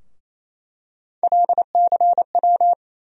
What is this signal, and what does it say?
Morse code, heard as a keyed tone, about 27 wpm, 700 Hz. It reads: LCW
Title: USA Call Sign Postfixes - 3 Letters 28wpm